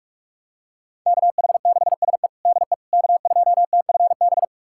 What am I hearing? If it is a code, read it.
Morse code, heard as a keyed tone, about 41 wpm, 700 Hz. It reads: KH6SE DE K2TFB
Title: USA Callsign DE Callsign - 5 Characters - 45wpm